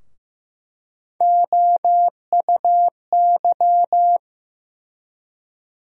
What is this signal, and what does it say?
Morse code, heard as a keyed tone, about 16 wpm, 700 Hz. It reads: OUY
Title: USA Call Sign Postfixes - 3 Letters 15wpm